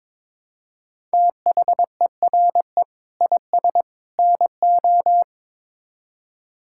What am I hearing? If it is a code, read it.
Morse code, heard as a keyed tone, about 22 wpm, 700 Hz. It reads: THERE IS NO
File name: practice / 22wpm / Build A Phrase - 5-Word Phrases - 22wpm